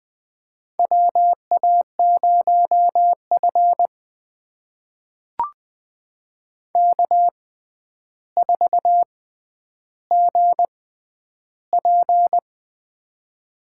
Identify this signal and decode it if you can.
Morse code, heard as a keyed tone, about 20 wpm, spaced as for 10 wpm, 700 Hz. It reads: WA0F K4GP
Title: USA Call Signs - 4 Characters - 6x Character Spacing - 20wpm